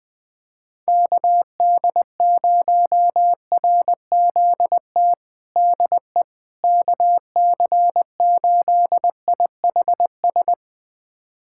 Morse code, heard as a keyed tone, about 20 wpm, 690 Hz. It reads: KD0RZT DE KC8IHS
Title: USA Callsign DE Callsign 20wpm